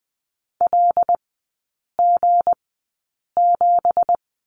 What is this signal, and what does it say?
Morse code, heard as a keyed tone, about 20 wpm, spaced as for 8 wpm, 690 Hz. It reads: LG7
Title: 166 Lesson 27e - Number 7 - Three Characters 20wpm